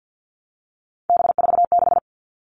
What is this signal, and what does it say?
Morse code, heard as a keyed tone, about 50 wpm, 700 Hz. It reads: B46
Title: All Cut Numbers - 3-Digit 50wpm